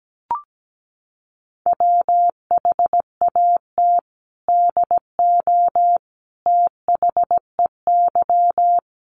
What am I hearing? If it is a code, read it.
Morse code, heard as a keyed tone, about 17 wpm, 700 Hz. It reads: WHAT DO THEY
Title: Sentences from Top 100 Words 17wpm